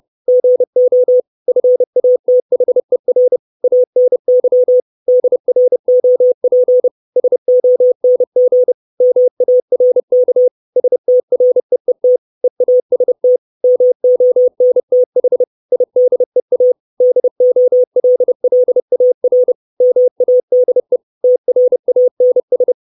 Morse code, heard as a keyed tone, about 30 wpm, 500 Hz. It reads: GO FATHER ANY DROP SONG MARK STREET EAST MONTH IDEA DOLLAR MADE TRANS
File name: 632 words / 632 Words 30WPM